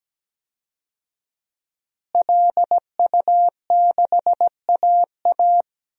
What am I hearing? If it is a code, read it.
Morse code, heard as a keyed tone, about 17 wpm, 700 Hz. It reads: LU6AA
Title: DXCC-100 Call Signs - 5 Characters - Repeated 3x - 17wpm